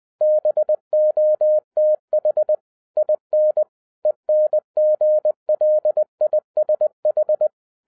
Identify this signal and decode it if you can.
Morse code, heard as a keyed tone, about 20 wpm, 610 Hz. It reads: BOTH IN ENGLISH
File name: 2012-01-mp3 / n05-CAME BACK TO HAM RADIO AFTER 30 YEARS OF INTERVAL-20wpm